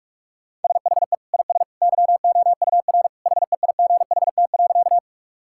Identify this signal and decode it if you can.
Morse code, heard as a keyed tone, about 47 wpm, 700 Hz. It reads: SHE IS YOUR HEIGHT.
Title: Sentences from Top 1600 Words 45wpm